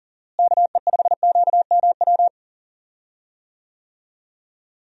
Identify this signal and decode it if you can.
Morse code, heard as a keyed tone, about 40 wpm, 700 Hz. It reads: KE5QMW
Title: USA Call Signs - 6 Characters - Repeated 3x - 40wpm